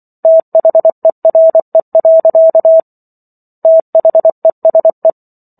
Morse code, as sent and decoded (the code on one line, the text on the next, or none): - .... .-. .-.-.- / - .... ... .
THERE. THESE